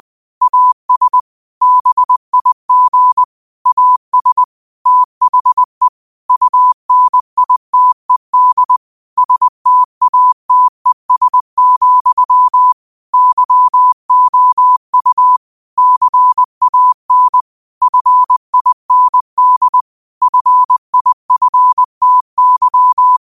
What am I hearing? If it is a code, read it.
Morse code, heard as a keyed tone, about 20 wpm, 1000 Hz. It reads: AS BIG AS THE UNITED STATES, YOU CAN FIND FIFTY
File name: qotdfiles / 20wpm